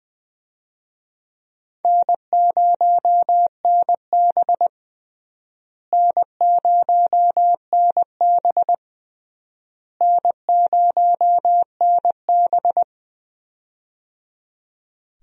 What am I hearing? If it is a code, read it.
Morse code, heard as a keyed tone, about 21 wpm, 700 Hz. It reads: N0NB N0NB N0NB
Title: USA Call Signs - 4 Characters - Repeated 3x - 20wpm